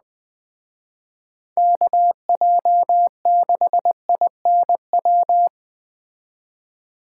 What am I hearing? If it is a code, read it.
Morse code, heard as a keyed tone, about 20 wpm, 700 Hz. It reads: KJ6INW
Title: Speed Racing - USA 2x3 Call Signs 17wpm